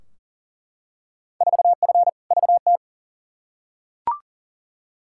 Morse code, heard as a keyed tone, about 40 wpm, 700 Hz. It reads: 4F VT
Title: Field Day Exchanges Encore - 40wpm